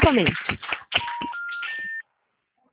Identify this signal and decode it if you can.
Special information tone, with speech.